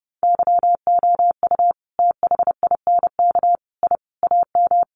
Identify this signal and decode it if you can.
Morse code, heard as a keyed tone, about 30 wpm, 700 Hz. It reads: YOU THINK I AM
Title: Sentences from Top 1200 Words 30wpm